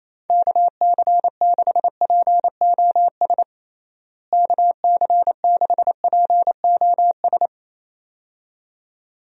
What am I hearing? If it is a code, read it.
Morse code, heard as a keyed tone, about 28 wpm, 700 Hz. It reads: KC6POS KC6POS
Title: USA Call Signs - 6 Characters - Repeated 2x - 28wpm